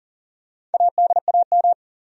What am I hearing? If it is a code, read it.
Morse code, heard as a keyed tone, about 40 wpm, 700 Hz. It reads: ADAM
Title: USA Common Names - Max 4 Letters - 40wpm